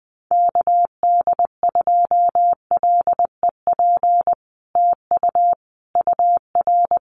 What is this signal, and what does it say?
Morse code, heard as a keyed tone, about 20 wpm, 700 Hz. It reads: KD2LEP TU UR
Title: Common QSO Phrases Encore 20wpm